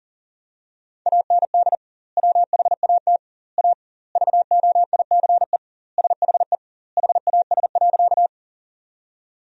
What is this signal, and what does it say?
Morse code, heard as a keyed tone, about 40 wpm, 700 Hz. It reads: AND WHAT A VOICE SHE HAS.
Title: 2x Word-Spacing - Sentences from Top 500 Words - 40wpm